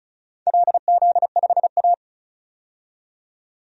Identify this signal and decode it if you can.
Morse code, heard as a keyed tone, about 36 wpm, 700 Hz. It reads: LZ5A
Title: DXCC-100 Call Signs - 4 Characters - 35wpm